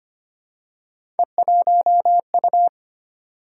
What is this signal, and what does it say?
Morse code, heard as a keyed tone, about 25 wpm, 700 Hz. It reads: E1U
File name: Three Letter-Number Combinations - 25wpm